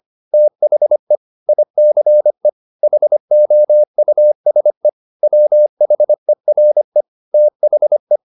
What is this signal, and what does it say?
Morse code, heard as a keyed tone, about 25 wpm, 600 Hz. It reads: THE ICE HOUSE WHERE THE
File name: the old man and the sea part 1 25wpm